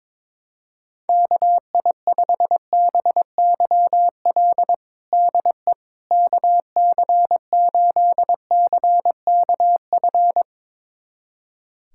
Morse code, heard as a keyed tone, about 22 wpm, 700 Hz. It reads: KI5BYL DE KC8CKF
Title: USA Callsign DE Callsign - 6 Characters - 22wpm